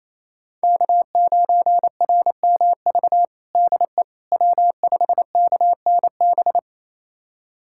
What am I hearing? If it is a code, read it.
Morse code, heard as a keyed tone, about 28 wpm, 700 Hz. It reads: K9RMV DE W5KNB
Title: USA Callsign DE Callsign - 5 Characters - 28wpm